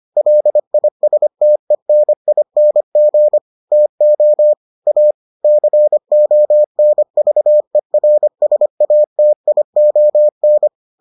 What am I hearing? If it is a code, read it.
Morse code, heard as a keyed tone, about 25 wpm, 600 Hz. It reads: LISTENING TO A CONVERSATION